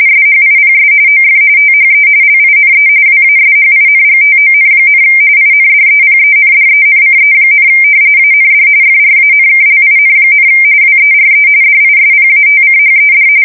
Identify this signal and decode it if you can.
RTTY, about 45.45 baud. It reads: SHIPS OF THE LINE WASHINGTON, INDEPENDENCE, AND FRANKLIN, WERE DISPATCHED
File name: fsk170 / usnavy2